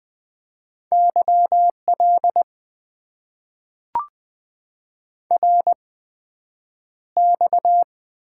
Morse code, heard as a keyed tone, about 20 wpm, spaced as for 9 wpm, 700 Hz. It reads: YL RX
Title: Top 90 QSO Elements - 8x Character Spacing - 20wpm